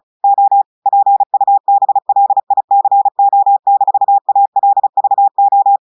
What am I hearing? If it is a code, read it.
Morse code, heard as a keyed tone, about 34 wpm, 800 Hz. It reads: O PUBLICO-ALVO
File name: portugais texte en clair 2 vitesse 35 mots minutes